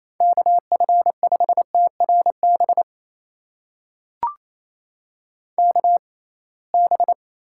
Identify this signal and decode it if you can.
Morse code, heard as a keyed tone, about 28 wpm, spaced as for 21 wpm, 700 Hz. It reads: KF5TRB KB
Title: USA Call Signs - 6 Characters - 6x Character Spacing - 28wpm